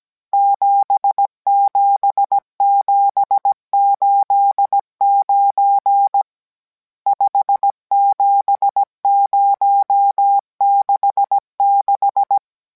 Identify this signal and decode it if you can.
Morse code, heard as a keyed tone, about 17 wpm, 800 Hz. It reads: 77789 57066